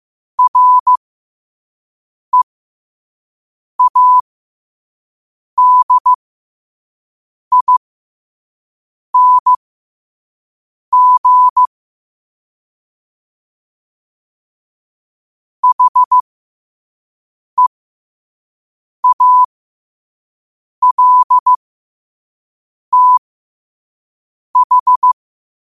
Morse code, heard as a keyed tone, about 15 wpm, spaced as for 5 wpm, 1000 Hz. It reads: READING HEALTH
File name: qotdfiles / 5wpm